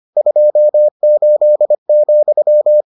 Morse code, heard as a keyed tone, about 25 wpm, 600 Hz. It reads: 28,